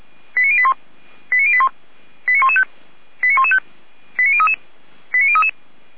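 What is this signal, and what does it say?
Five-tone selective call (ZVEI), heard as quick runs of five tones, tones of about 70 ms. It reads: 89081 89081 89115 89115 89033 89033
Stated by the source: ZVEI five-tone selective call, received by SDR.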